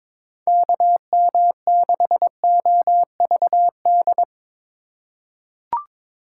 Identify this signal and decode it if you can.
Morse code, heard as a keyed tone, about 22 wpm, 700 Hz. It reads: KM6OVD
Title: USA Call Signs - 6 Characters - 3x Character Spacing - 22wpm